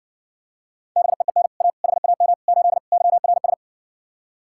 Morse code, heard as a keyed tone, about 66 wpm, 700 Hz. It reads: BEEN A HAM 8 YRS